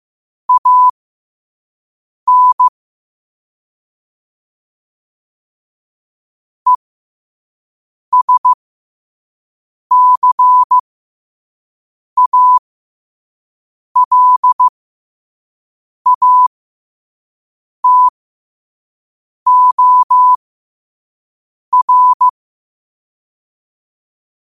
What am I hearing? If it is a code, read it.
Morse code, heard as a keyed tone, about 15 wpm, spaced as for 5 wpm, 1000 Hz. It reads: AN ESCALATOR
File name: qotdfiles / 5wpm